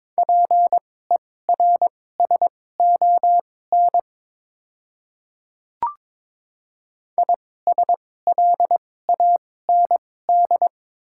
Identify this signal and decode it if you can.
Morse code, heard as a keyed tone, about 22 wpm, spaced as for 16 wpm, 700 Hz. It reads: PERSON ISLAND
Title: 2x Character Spacing - Top 500 Words - 22wpm